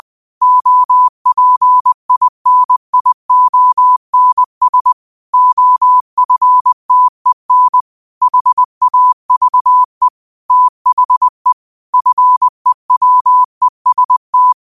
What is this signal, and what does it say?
Morse code, heard as a keyed tone, about 20 wpm, 1000 Hz. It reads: OPINIONS OFTEN HAVE THE FEWEST
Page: New quotes every day in morse code at 20 Words per minute.